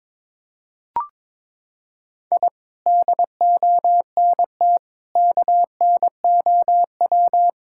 Morse code, heard as a keyed tone, about 21 wpm, 700 Hz. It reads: I DONT KNOW
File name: practice / 22wpm / Sentences with 8-Word Phrases - 22wpm